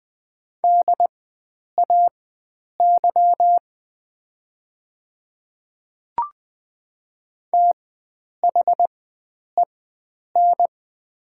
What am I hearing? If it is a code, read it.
Morse code, heard as a keyed tone, about 20 wpm, spaced as for 9 wpm, 690 Hz. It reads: DAY THEN
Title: 4x Character Spacing - Top 100 Words - 20wpm